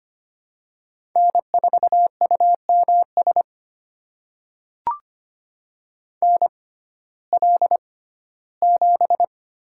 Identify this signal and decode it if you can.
Morse code, heard as a keyed tone, about 25 wpm, spaced as for 15 wpm, 700 Hz. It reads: N4UMS NL7
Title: USA Call Signs - 5 Characters - 6x Character Spacing - 25wpm